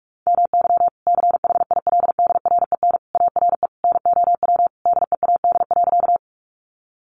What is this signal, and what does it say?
Morse code, heard as a keyed tone, about 44 wpm, 700 Hz. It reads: MY CHILDREN ARE NOW DEAD.
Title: Sentences from Top 1500 Words 45wpm